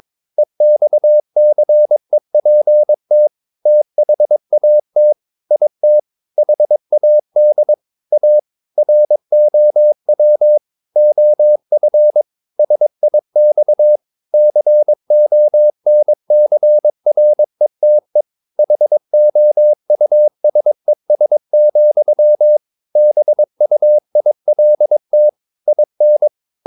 Morse code, heard as a keyed tone, about 22 wpm, 600 Hz. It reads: EXCEPT THAT IT HAD A ROW OF SIX CONCRETE HOUSES, BUILT IN
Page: Romanen Never af Ken Follet på engelsk.